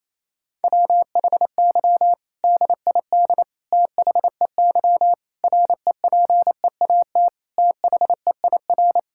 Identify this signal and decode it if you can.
Morse code, heard as a keyed tone, about 28 wpm, 700 Hz. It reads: WHY DID THEY REPEAT THEIR
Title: Sentences from Top 1600 Words 28wpm